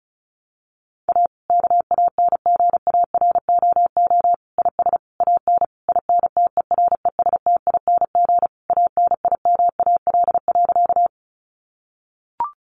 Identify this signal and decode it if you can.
Morse code, heard as a keyed tone, about 34 wpm, 700 Hz. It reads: A KANGAROO IS AN INTERESTING ANIMAL.
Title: Easy Phrases 35wpm